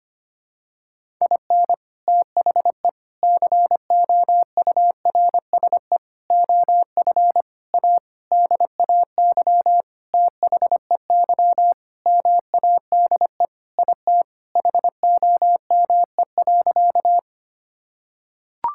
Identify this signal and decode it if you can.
Morse code, heard as a keyed tone, about 25 wpm, 700 Hz. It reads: IN THE COURSE OF A DAY THEY MADE IT HOME.
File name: practice / 25wpm / Sentences with 4-Word Phrases - 25wpm